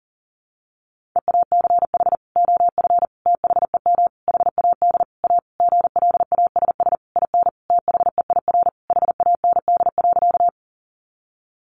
Morse code, heard as a keyed tone, about 40 wpm, 700 Hz. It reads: EACH OF THEM HAD A GLASS IN THEIR HAND.
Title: Sentences from Top 1300 Words 40wpm